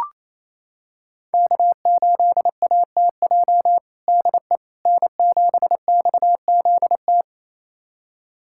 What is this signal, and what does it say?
Morse code, heard as a keyed tone, about 27 wpm, 700 Hz. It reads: K8ATJ DE N7XZT
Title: USA Callsign DE Callsign - 5 Characters - 28wpm